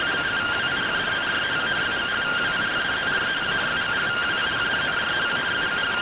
Two-tone teleprinter signal, about 170 Hz shift.